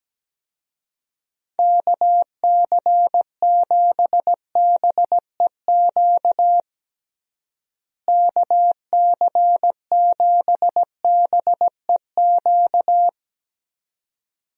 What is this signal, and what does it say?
Morse code, heard as a keyed tone, about 17 wpm, 700 Hz. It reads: KC7BEQ KC7BEQ
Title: USA Call Signs - 6 Characters - Repeated 2x - 17wpm